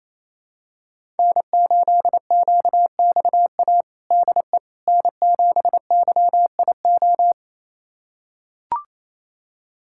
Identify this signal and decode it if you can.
Morse code, heard as a keyed tone, about 28 wpm, 700 Hz. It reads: N8QXA DE N7YIO
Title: USA Callsign DE Callsign - 5 Characters - 28wpm